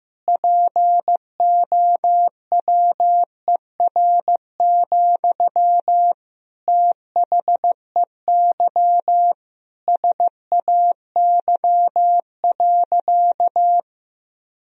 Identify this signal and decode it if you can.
Morse code, heard as a keyed tone, about 15 wpm, 700 Hz. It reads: POWER, THEY SAY.
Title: Sentences from Top 1000 Words - Review Part 2 - 15wpm